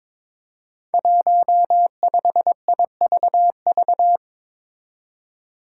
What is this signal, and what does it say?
Morse code, heard as a keyed tone, about 22 wpm, 700 Hz. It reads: 15IVV
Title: Five Challenging Letter-Number Combinations 22wpm